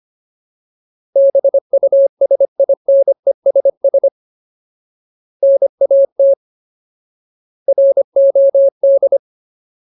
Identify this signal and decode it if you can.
Morse code, heard as a keyed tone, about 25 wpm, 550 Hz. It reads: BUSINESS NAT ROD
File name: Ultimate Word List / 25WPM Ultimate